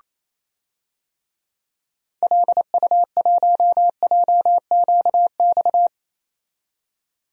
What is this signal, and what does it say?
Morse code, heard as a keyed tone, about 27 wpm, 700 Hz. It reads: LU1JQX
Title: DXCC-100 Call Signs - 6 Characters - 28wpm